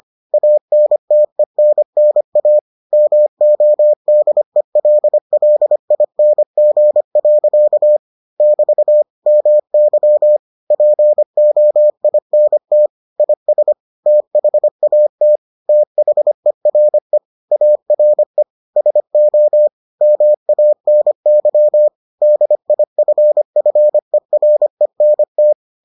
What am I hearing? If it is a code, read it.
Morse code, heard as a keyed tone, about 25 wpm, 600 Hz. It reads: ANTENNA MODELLING. <BT> MY POINT IS THAT THERE ARE SO MANY DIFFERENT